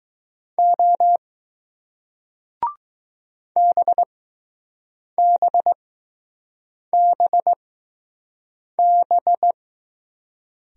Morse code, heard as a keyed tone, about 20 wpm, spaced as for 6 wpm, 700 Hz. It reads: O BBBB